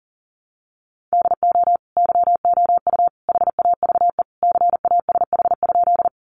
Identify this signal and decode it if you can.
Morse code, heard as a keyed tone, about 40 wpm, 700 Hz. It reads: DO YOU HAVE CASH?
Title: Sentences from Top 700 Words 40wpm